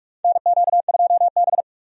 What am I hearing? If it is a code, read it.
Morse code, heard as a keyed tone, about 46 wpm, 700 Hz. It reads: NQ2B
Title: USA Call Signs - 4 Characters - 45wpm